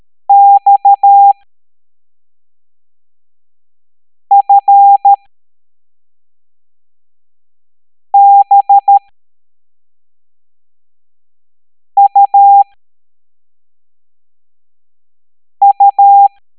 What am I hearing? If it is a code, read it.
Morse code, heard as a keyed tone, about 13 wpm, spaced as for 3 wpm, 800 Hz. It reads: XFBUU
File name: hamdata / L09-EX04